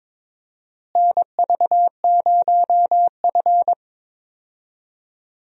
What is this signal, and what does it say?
Morse code, heard as a keyed tone, about 22 wpm, 700 Hz. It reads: NV0F
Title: USA Call Signs - 4 Characters - Repeated 3x - 22wpm